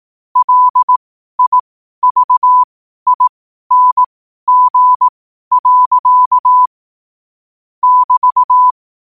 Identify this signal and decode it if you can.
Morse code, heard as a keyed tone, about 18 wpm, spaced as for 12 wpm, 1000 Hz. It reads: LIVING. <BT>
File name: qotdfiles / 12wpm